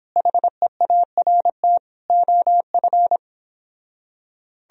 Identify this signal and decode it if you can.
Morse code, heard as a keyed tone, about 29 wpm, 700 Hz. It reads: HEART OF